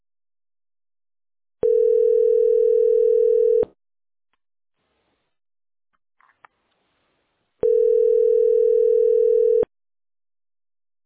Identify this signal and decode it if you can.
Ringback tone (North America), 440 + 480 Hz.